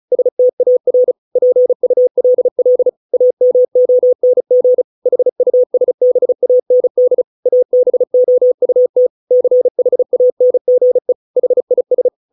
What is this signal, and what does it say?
Morse code, heard as a keyed tone, about 36 wpm, 500 Hz. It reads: STAR PULL AMONG HUSBAND ABOUT CHANGE HIS